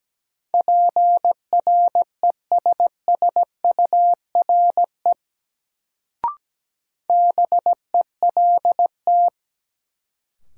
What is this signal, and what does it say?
Morse code, heard as a keyed tone, about 17 wpm, 700 Hz. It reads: PRESSURE BELT
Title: Sentences from Top 1400 Words - New Word Drill Alt - 17wpm